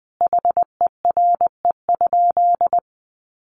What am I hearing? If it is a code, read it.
Morse code, heard as a keyed tone, about 20 wpm, 700 Hz. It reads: HERE?
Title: Sentences from Top 1400 Words 20wpm